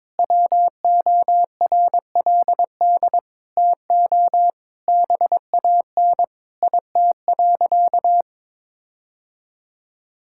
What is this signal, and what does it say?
Morse code, heard as a keyed tone, about 22 wpm, 700 Hz. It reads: WORLD TO BAN IT.
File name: practice / 22wpm / Sentences with 8-Word Phrases - 22wpm